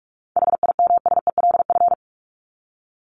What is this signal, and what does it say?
Morse code, heard as a keyed tone, about 44 wpm, 700 Hz. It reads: HIMSELF
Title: Top 1501-1750 Words 45wpm